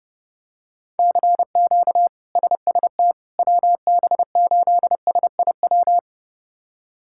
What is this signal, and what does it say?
Morse code, heard as a keyed tone, about 30 wpm, 700 Hz. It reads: CQ SST WB8SIW
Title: K1USN SST QSO Practice - 30wpm